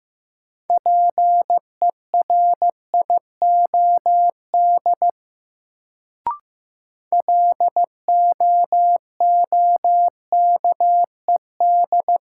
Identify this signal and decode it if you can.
Morse code, heard as a keyed tone, about 15 wpm, 700 Hz. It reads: PERIOD LOOKED
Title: Sentences from Top 800 Words - New Word Drill - 15wpm